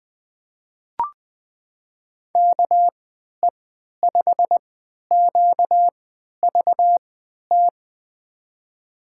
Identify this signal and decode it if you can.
Morse code, heard as a keyed tone, about 21 wpm, spaced as for 12 wpm, 700 Hz. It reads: KE5QVT
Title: USA Call Signs - 6 Characters - 3x Character Spacing - 20wpm